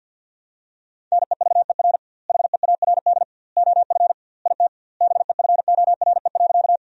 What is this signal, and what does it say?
Morse code, heard as a keyed tone, about 50 wpm, 700 Hz. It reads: NEVER HEARD OF IT BEFORE.
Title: 2x Word-Spacing - Sentences from Top 400 Words - 50wpm